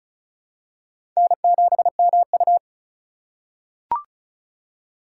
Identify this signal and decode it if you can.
Morse code, heard as a keyed tone, about 34 wpm, 700 Hz. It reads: N7MU
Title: USA Call Signs - 3x Character Spacing - 35wpm